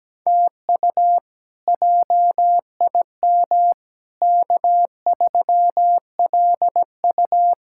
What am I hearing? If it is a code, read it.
Morse code, heard as a keyed tone, about 17 wpm, 700 Hz. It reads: TU JIM K3LU
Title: K1USN SST QSO Practice - 17wpm